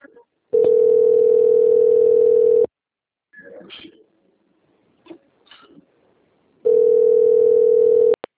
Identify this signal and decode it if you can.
Ringback tone (North America), 440 + 480 Hz, with speech.
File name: out- - - - - .12768